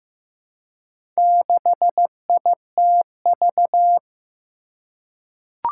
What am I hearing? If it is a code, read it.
Morse code, heard as a keyed tone, about 15 wpm, 700 Hz. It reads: 6ITV
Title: Four Challenging Letter-Number Combinations 15wpm